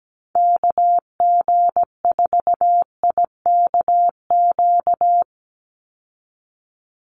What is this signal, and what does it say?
Morse code, heard as a keyed tone, about 17 wpm, 700 Hz. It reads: KG4IKQ
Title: Call Signs Repeated 2x - 17wpm